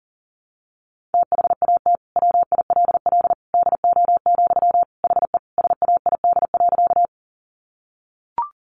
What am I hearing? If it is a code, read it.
Morse code, heard as a keyed tone, about 40 wpm, 700 Hz. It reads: THAT WILL DO, HE SAID.
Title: Sentences from Top 100 Words 40wpm